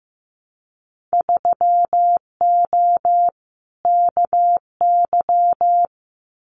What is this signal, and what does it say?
Morse code, heard as a keyed tone, about 15 wpm, 700 Hz. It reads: 3O KY
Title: Winter Field Day Exchanges 15wpm